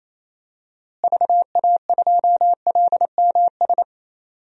Morse code, heard as a keyed tone, about 28 wpm, 700 Hz. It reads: VA2LMS
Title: DXCC-100 Call Signs - 6 Characters - Repeated 2x - 28wpm